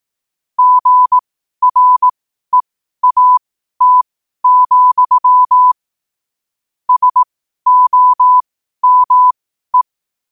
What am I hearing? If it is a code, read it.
Morse code, heard as a keyed tone, about 18 wpm, spaced as for 12 wpm, 1000 Hz. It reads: GREAT, SOME
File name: qotdfiles / 12wpm